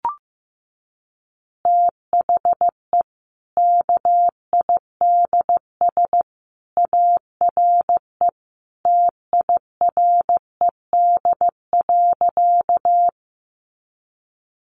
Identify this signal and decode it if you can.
Morse code, heard as a keyed tone, about 15 wpm, 700 Hz. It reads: THE KIDS ARE TIRED.
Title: Easy Phrases 15wpm